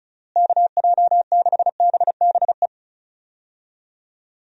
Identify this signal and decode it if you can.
Morse code, heard as a keyed tone, about 36 wpm, 700 Hz. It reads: KJ6BBE
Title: USA Call Signs - 6 Characters - Repeated 3x - 35wpm